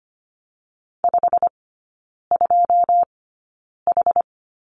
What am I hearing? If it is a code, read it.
Morse code, heard as a keyed tone, about 25 wpm, spaced as for 9 wpm, 700 Hz. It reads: 52H
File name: Lesson 17e - Letter u - Three Characters 25wpm